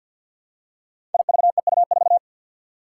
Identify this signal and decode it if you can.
Morse code, heard as a keyed tone, about 52 wpm, 700 Hz. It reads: IVEU4
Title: Five Challenging Letter-Number Combinations 50wpm